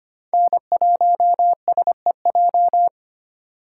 Morse code, heard as a keyed tone, about 32 wpm, 700 Hz. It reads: N1SEJ